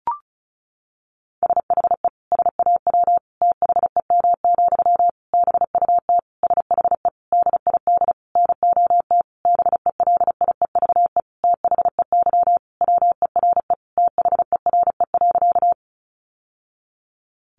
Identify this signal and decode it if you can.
Morse code, heard as a keyed tone, about 34 wpm, 700 Hz. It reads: SHE SAW THEM, BUT SHE DID NOT BELIEVE THEY WERE THERE.
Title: Sentences from Top 1000 Words 35wpm